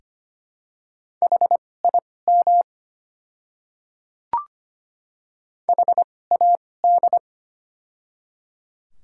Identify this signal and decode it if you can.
Morse code, heard as a keyed tone, about 25 wpm, spaced as for 18 wpm, 700 Hz. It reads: HIM HAD
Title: 2x Character Spacing - Top 100 Words - 25wpm